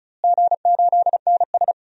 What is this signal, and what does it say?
Morse code, heard as a keyed tone, about 36 wpm, 690 Hz. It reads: G8NS
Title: DXCC-100 Call Signs - 4 Characters - Repeated 2x - 35wpm